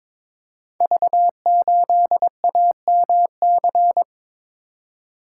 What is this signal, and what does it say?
Morse code, heard as a keyed tone, about 23 wpm, 700 Hz. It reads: V8AMC
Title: Code Groups - 5 Letter:Numbers - 22wpm